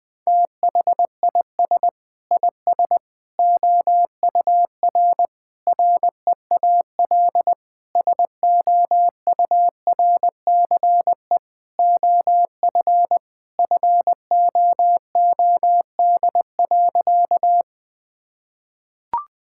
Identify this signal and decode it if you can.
Morse code, heard as a keyed tone, about 21 wpm, 700 Hz. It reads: THIS IS OUR REAL SOURCE OF FOOD.
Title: Sentences from Top 1000 Words - Review Part 5 - 20wpm